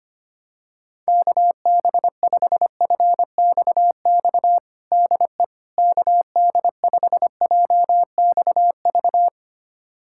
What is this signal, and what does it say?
Morse code, heard as a keyed tone, about 25 wpm, 700 Hz. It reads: KB5FXX DE KD5JXV
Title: USA Callsign DE Callsign - 6 Characters - 25wpm